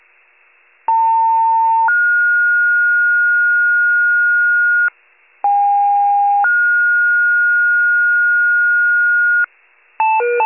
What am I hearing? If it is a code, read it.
Two-tone page, heard as a one-second tone, then a three-second tone, about 900 then 1430 Hz, about 810 then 1430 Hz. It is heard from 0.9 to 9.5 s.
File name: pager / 78 FIRE / wcalert